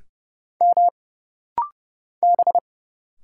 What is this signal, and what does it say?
Morse code, heard as a keyed tone, about 30 wpm, 700 Hz. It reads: M B
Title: Single Letter-Number Alt - Rapid-Fire 30wpm